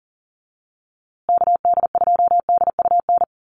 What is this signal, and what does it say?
Morse code, heard as a keyed tone, about 40 wpm, 700 Hz. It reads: KD2DUN
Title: USA Call Signs Repeated 2x 40wpm